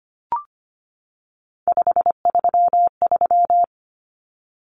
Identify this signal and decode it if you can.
Morse code, heard as a keyed tone, about 25 wpm, 700 Hz. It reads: 533
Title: Numbers - 3-Digit 25wpm